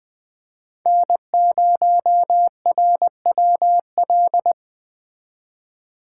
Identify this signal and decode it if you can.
Morse code, heard as a keyed tone, about 20 wpm, 700 Hz. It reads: N0RWL
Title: Speed Racing - USA 1x3 Call Signs 17wpm